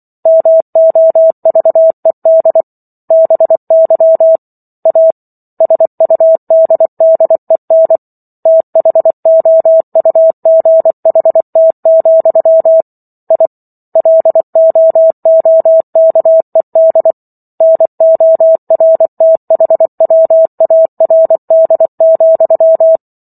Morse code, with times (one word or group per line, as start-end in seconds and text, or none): 0.3-2.6 MOVED
3.1-4.4 BY
4.9-5.1 A
5.6-8.0 SUDDEN
8.5-12.8 THOUGHT,
13.3-13.5 I
14.0-17.1 LOOKED
17.6-23.0 NORTHWARD,